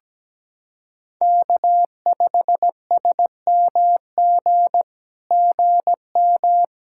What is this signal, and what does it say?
Morse code, heard as a keyed tone, about 17 wpm, 700 Hz. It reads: K5SMG GM
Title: SOTA QSO Practice 17wpm